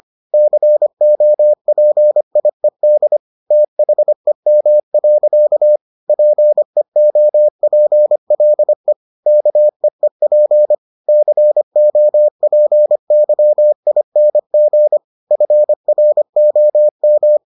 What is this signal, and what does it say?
Morse code, heard as a keyed tone, about 25 wpm, 600 Hz. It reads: COPIED THEM. PEOPLE KEEP COPYING FROM